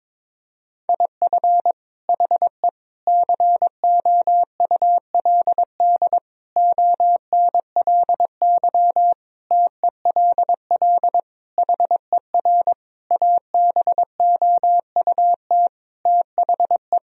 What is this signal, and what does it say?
Morse code, heard as a keyed tone, about 22 wpm, 700 Hz. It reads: IF HE COULD ONLY TELL HER ABOUT THE
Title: Sentences from Top 1000 Words - Review Part 1 - 22wpm